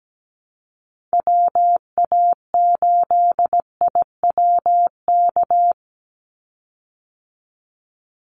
Morse code, heard as a keyed tone, about 17 wpm, 700 Hz. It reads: WA8IWK
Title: Call Signs Repeated 2x - 17wpm